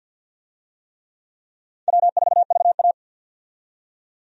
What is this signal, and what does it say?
Morse code, heard as a keyed tone, about 52 wpm, 700 Hz. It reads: W4VA
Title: USA Call Signs - 4 Characters - 50wpm